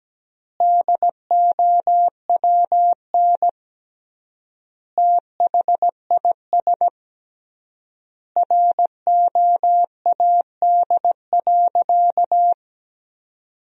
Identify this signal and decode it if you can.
Morse code, heard as a keyed tone, about 17 wpm, 700 Hz. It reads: DOWN THIS ROAD.
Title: 3x Word-Spacing - Sentences from Top 400 Words - 17wpm